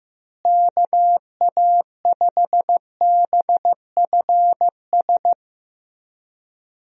Morse code, heard as a keyed tone, about 15 wpm, 700 Hz. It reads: KA5BFS
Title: USA Call Signs - 6 Characters - Repeated 2x - 15wpm